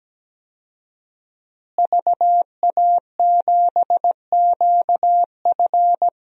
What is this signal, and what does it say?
Morse code, heard as a keyed tone, about 17 wpm, 700 Hz. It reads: VA7QF
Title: DXCC-100 Call Signs - 5 Characters - Repeated 3x - 17wpm